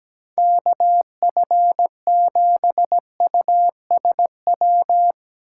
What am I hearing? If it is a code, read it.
Morse code, heard as a keyed tone, about 17 wpm, 700 Hz. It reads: KF7USW
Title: USA Call Signs - 6 Characters - 3x Character Spacing - 17wpm